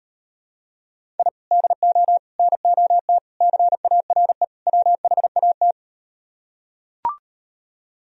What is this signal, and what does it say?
Morse code, heard as a keyed tone, about 38 wpm, 700 Hz. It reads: I DO NOT CARE WHAT